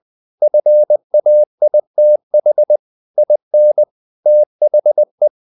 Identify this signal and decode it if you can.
Morse code, heard as a keyed tone, about 20 wpm, 600 Hz. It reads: FAITH IN THE
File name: the old man and the sea / the old man and the sea part 1 20wpm